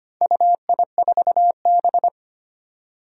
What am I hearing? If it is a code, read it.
Morse code, heard as a keyed tone, about 25 wpm, 700 Hz. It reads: UI4B
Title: Four Challenging Letter-Number Combinations 25wpm